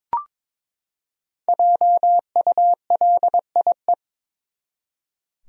Morse code, heard as a keyed tone, about 21 wpm, 700 Hz. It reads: JULIE
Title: USA Common Names - Max 5 Letters - 22wpm